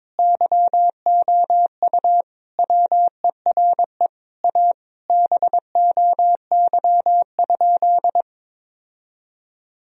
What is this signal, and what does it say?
Morse code, heard as a keyed tone, about 22 wpm, 700 Hz. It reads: YOU WERE A BOY?
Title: Sentences from Top 1000 Words - Review Part 1 - 22wpm